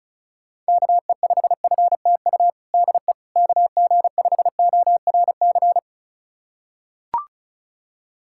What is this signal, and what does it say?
Morse code, heard as a keyed tone, about 36 wpm, 700 Hz. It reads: KE5FTU DE KG5ORC
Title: USA Callsign DE Callsign 35wpm